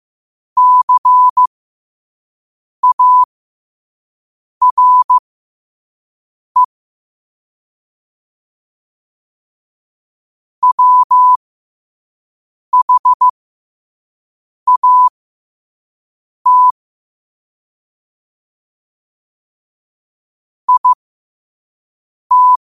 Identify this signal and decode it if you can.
Morse code, heard as a keyed tone, about 15 wpm, spaced as for 5 wpm, 1000 Hz. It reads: CARE WHAT IT
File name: qotdfiles / 5wpm